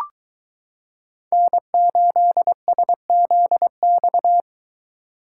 Morse code, heard as a keyed tone, about 22 wpm, 700 Hz. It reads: N8SZX